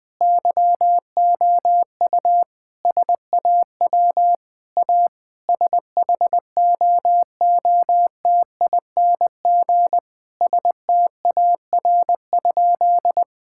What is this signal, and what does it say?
Morse code, heard as a keyed tone, about 21 wpm, 700 Hz. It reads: YOU SAW A SHOOTING STAR?
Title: Sentences with 8-Word Phrases - Spoken Beginning - 20wpm